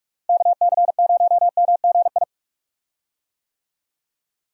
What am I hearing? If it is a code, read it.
Morse code, heard as a keyed tone, about 44 wpm, 700 Hz. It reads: KC0MGI
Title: USA Call Signs - 6 Characters - Repeated 3x - 45wpm